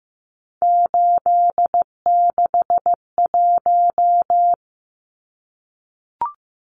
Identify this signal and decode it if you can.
Morse code, heard as a keyed tone, about 15 wpm, 700 Hz. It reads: 861
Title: Numbers - 3-Digit 15wpm